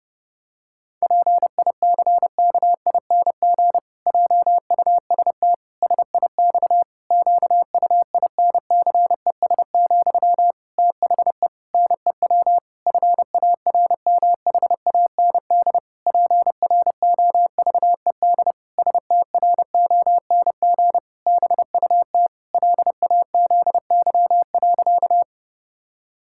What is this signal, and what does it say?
Morse code, heard as a keyed tone, about 32 wpm, 700 Hz. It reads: PICKING JUST SIX QUINCES, THE NEW FARMHAND PROVED STRONG BUT LAZY.
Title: Pangram Challenge v2 - 30wpm